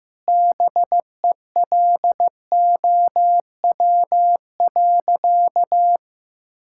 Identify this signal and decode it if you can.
Morse code, heard as a keyed tone, about 15 wpm, 700 Hz. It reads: BELOW.
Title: Sentences from Top 1000 Words - Review Part 5 - 15wpm